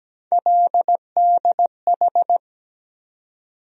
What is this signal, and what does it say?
Morse code, heard as a keyed tone, about 17 wpm, 700 Hz. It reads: LDH
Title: USA Call Sign Postfixes - 3 Letters 17wpm